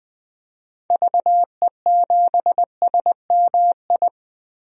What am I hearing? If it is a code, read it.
Morse code, heard as a keyed tone, about 20 wpm, 700 Hz. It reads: VE7SMI
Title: DXCC-100 Call Signs - 6 Characters - Repeated 2x - 20wpm